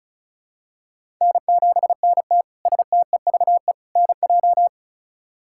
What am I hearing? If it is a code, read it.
Morse code, heard as a keyed tone, about 34 wpm, 700 Hz. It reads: N7NT STEVE NJ
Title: K1USN SST QSO Practice - 35wpm